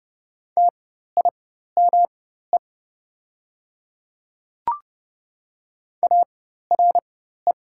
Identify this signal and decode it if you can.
Morse code, heard as a keyed tone, about 32 wpm, spaced as for 14 wpm, 700 Hz. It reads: TIME ARE
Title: 4x Character Spacing - Top 300 Words - 30wpm